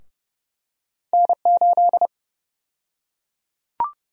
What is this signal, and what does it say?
Morse code, heard as a keyed tone, about 31 wpm, 700 Hz. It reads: N8
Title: USA Call Sign Prefixes - 30wpm